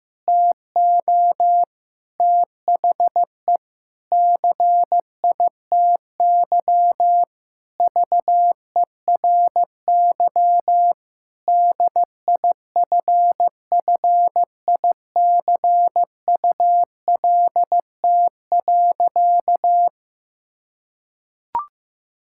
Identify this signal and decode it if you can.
Morse code, heard as a keyed tone, about 16 wpm, 700 Hz. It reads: TO THE CITY VERY DIFFICULT.
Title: Sentences from Top 1000 Words - Review Part 6 - 15wpm